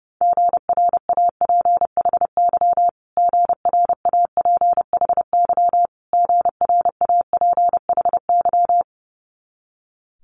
Morse code, heard as a keyed tone, about 30 wpm, 700 Hz. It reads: GRAPHY GRAPHY GRAPHY
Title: Multi-letter Phonemes Repeated 3x 30wpm